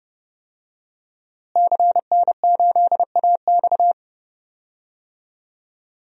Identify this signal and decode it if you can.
Morse code, heard as a keyed tone, about 30 wpm, 700 Hz. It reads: CN8AX
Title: DXCC-100 Call Signs - 5 Characters - Repeated 2x - 30wpm